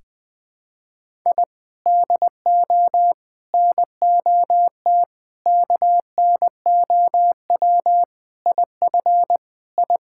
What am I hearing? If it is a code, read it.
Morse code, heard as a keyed tone, about 20 wpm, 700 Hz. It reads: I DO NOT KNOW IF I
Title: Sentences with 5-Word Phrases - Spoken Beginning - 20wpm